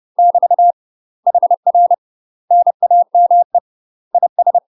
Morse code, heard as a keyed tone, about 30 wpm, 700 Hz. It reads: <BT> HR NAME IS
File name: UK MASTER QSOs TRAINING / 30WPM / UK MASTER QSOs 0044